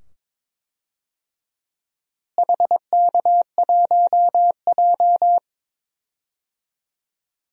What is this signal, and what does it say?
Morse code, heard as a keyed tone, about 22 wpm, 700 Hz. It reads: HK1J
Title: DXCC-100 Call Signs - 4 Characters - Repeated 2x - 22wpm